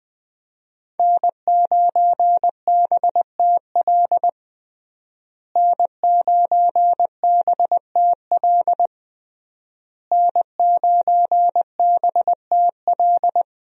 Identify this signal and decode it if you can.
Morse code, heard as a keyed tone, about 21 wpm, 700 Hz. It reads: N9BTL N9BTL N9BTL
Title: USA Call Signs - 5 Characters - Repeated 3x - 20wpm